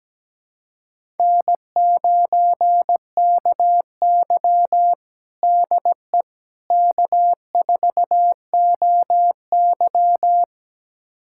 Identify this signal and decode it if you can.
Morse code, heard as a keyed tone, about 17 wpm, 700 Hz. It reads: N9KY DE K4OY
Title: USA Callsign DE Callsign - 4 Characters - 17wpm